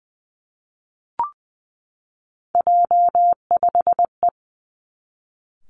Morse code, heard as a keyed tone, about 20 wpm, 700 Hz. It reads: J5E
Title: Three Challenging Letter-Number Combinations 20wpm